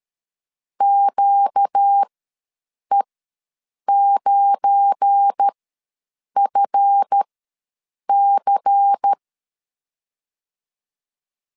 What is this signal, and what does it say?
Morse code, heard as a keyed tone, about 13 wpm, spaced as for 7 wpm, 800 Hz. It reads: QE9FC
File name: LEZIONI / 7 GF9Z / 7 GF9 ws 12wpm 3344